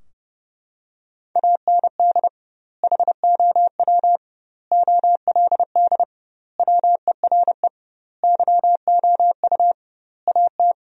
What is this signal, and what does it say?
Morse code, heard as a keyed tone, about 30 wpm, 700 Hz. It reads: AND HOW OLD WERE YOU AT
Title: 2x Word-Spacing - Sentences from Top 200 Words - 30wpm